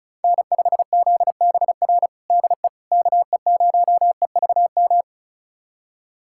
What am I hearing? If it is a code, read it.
Morse code, heard as a keyed tone, about 34 wpm, 700 Hz. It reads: N5ZBR DE KE0EVM
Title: USA Callsign DE Callsign 35wpm